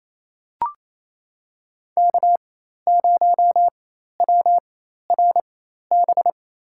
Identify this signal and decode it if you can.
Morse code, heard as a keyed tone, about 28 wpm, spaced as for 13 wpm, 700 Hz. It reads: K0WRB
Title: USA Call Signs - 4x Character Spacing - 28wpm